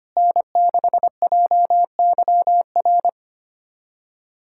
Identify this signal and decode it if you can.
Morse code, heard as a keyed tone, about 25 wpm, 700 Hz. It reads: N6JYR
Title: USA Callsign DE Callsign 25wpm